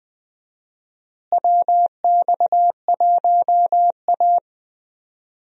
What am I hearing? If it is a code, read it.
Morse code, heard as a keyed tone, about 21 wpm, 700 Hz. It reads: WX1A
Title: USA Call Signs - 4 Characters - Repeated 2x - 20wpm